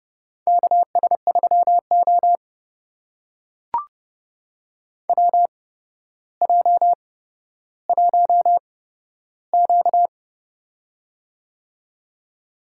Morse code, heard as a keyed tone, about 30 wpm, spaced as for 13 wpm, 700 Hz. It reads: KS3O WJ1Q
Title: USA Call Signs - 4 Characters - 8x Character Spacing - 30wpm